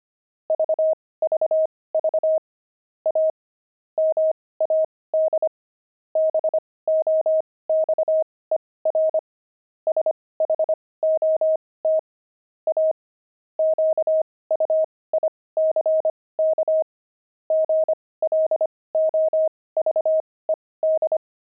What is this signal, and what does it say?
Morse code, heard as a keyed tone, about 25 wpm, spaced as for 18 wpm, 630 Hz. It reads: VVV A MAD BOXER SHOT A QUICK GLOVED
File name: pangram-fs-18-cs-25-gloved